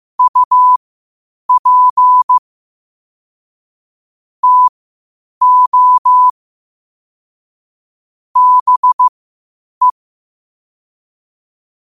Morse code, heard as a keyed tone, about 15 wpm, spaced as for 8 wpm, 1000 Hz. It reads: UP TO BE
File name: qotdfiles / 8wpm